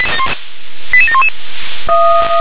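Five-tone selective call (ZVEI), heard as quick runs of five tones, tones of about 70 ms. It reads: 88711
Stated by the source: ZVEI five-tone call 88747 received on an SDR.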